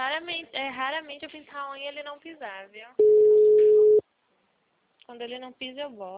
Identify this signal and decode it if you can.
Ringback tone (Europe, 425 Hz), with speech.
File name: out- -2035- - - .602